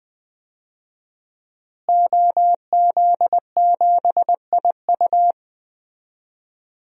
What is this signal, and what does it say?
Morse code, heard as a keyed tone, about 20 wpm, 700 Hz. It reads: OZ7IU
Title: DXCC-100 Call Signs - 5 Characters - Repeated 2x - 20wpm